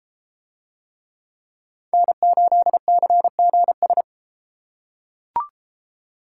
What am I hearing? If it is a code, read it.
Morse code, heard as a keyed tone, about 31 wpm, 700 Hz. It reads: N8CGS